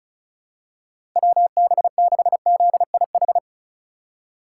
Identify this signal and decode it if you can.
Morse code, heard as a keyed tone, about 34 wpm, 700 Hz. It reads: WB6ZIH
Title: USA Call Signs - 6 Characters - 3x Character Spacing - 35wpm